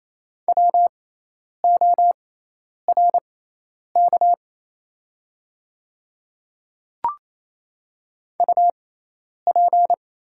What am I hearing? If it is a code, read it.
Morse code, heard as a keyed tone, about 28 wpm, spaced as for 10 wpm, 700 Hz. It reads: WORK UP
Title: 6x Character Spacing - Top 200 Words - 28wpm